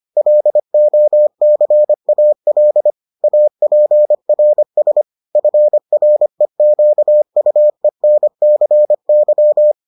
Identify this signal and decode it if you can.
Morse code, heard as a keyed tone, about 25 wpm, 600 Hz. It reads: LOCAL APRS FREQUENCY